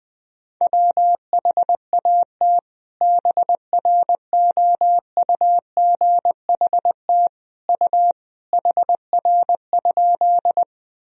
Morse code, heard as a keyed tone, about 20 wpm, 700 Hz. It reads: WHAT BROUGHT U HR?
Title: Sentences from Top 500 Words and Abbreviations 20wpm